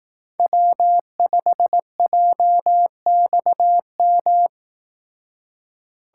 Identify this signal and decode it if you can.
Morse code, heard as a keyed tone, about 18 wpm, 700 Hz. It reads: W5JXM
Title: Speed Racing - USA 1x3 Call Signs 15wpm